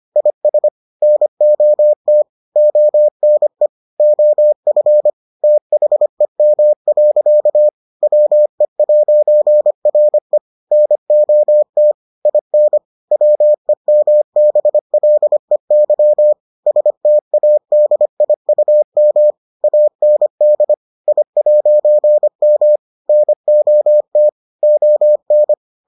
Morse code, heard as a keyed tone, about 25 wpm, 600 Hz. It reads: IS NOT ONE OF THEM. WE'RE NOT IN WEMBLEY STADIUM AND I'M NOT ON